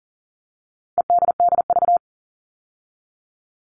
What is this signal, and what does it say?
Morse code, heard as a keyed tone, about 40 wpm, 700 Hz. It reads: EDDV
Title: All Cut Numbers - 4-Digit 40wpm